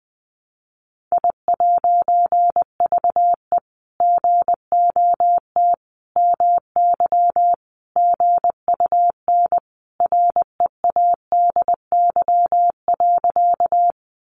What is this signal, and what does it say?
Morse code, heard as a keyed tone, about 20 wpm, 700 Hz. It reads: I'VE GOT MY GUN READY.
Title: A Sound Of Thunder 20wpm